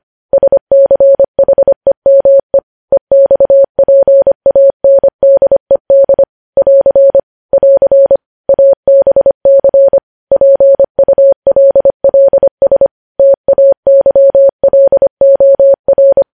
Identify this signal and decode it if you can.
Morse code, heard as a keyed tone, about 25 wpm, 550 Hz. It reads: SCHEME EXPANDED <AR> <AR> ABC PULLS TAYLOR
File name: podcast / News headlines-1x-25x25